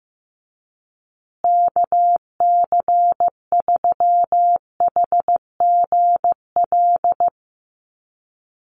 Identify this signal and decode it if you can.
Morse code, heard as a keyed tone, about 15 wpm, 700 Hz. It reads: KC3HGL
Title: USA Call Signs Repeated 2x 15wpm